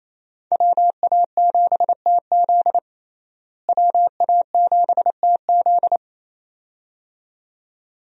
Morse code, heard as a keyed tone, about 28 wpm, 700 Hz. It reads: WA7TZ WA7TZ
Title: USA Call Signs - 5 Characters - Repeated 2x - 28wpm